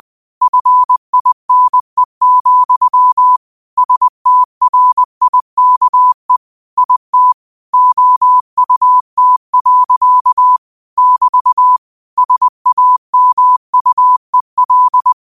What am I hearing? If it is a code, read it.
Morse code, heard as a keyed tone, about 20 wpm, 1000 Hz. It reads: FINE, STRIKE IT OUT. <BT> SAMUEL